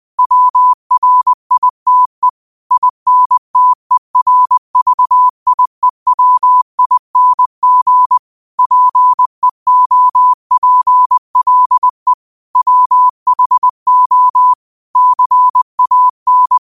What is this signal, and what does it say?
Morse code, heard as a keyed tone, about 20 wpm, 1000 Hz. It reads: WRITE INTERVIEWING PEOPLE WHO CAN
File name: qotdfiles / 20wpm